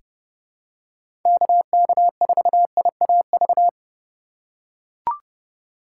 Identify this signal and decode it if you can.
Morse code, heard as a keyed tone, about 30 wpm, 700 Hz. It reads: KK4IAV
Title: USA Call Signs - 6 Characters - 3x Character Spacing - 30wpm